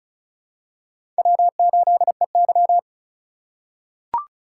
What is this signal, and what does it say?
Morse code, heard as a keyed tone, about 34 wpm, 700 Hz. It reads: W8EY
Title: USA Call Signs - 4 Characters - 3x Character Spacing - 35wpm